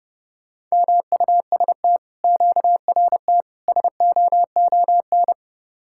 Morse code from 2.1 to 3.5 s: QRT

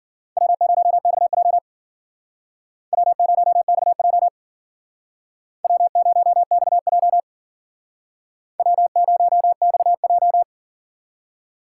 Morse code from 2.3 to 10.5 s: W0XJ W0XJ W0XJ

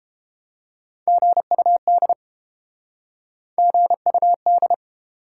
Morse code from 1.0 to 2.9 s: GUD

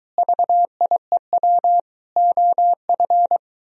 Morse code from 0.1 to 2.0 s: VIEW

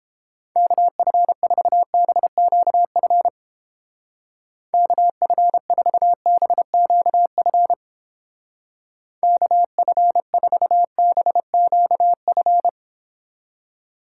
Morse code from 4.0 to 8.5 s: KF4BQF

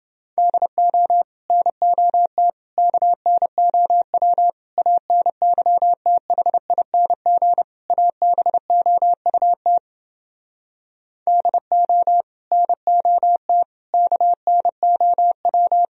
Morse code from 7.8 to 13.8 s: ABOUT DO NOT

